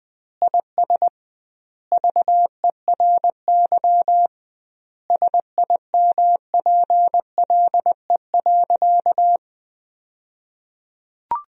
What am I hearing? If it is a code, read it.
Morse code, heard as a keyed tone, about 21 wpm, 700 Hz. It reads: IS VERY SIMPLE.
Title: 2x Word-Spacing - Sentences from Top 500 Words - 20wpm